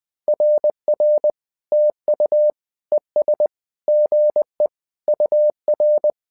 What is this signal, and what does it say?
Morse code, heard as a keyed tone, about 20 wpm, 600 Hz. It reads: RR TU ES GE UR
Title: SOTA QSO Practice 20wpm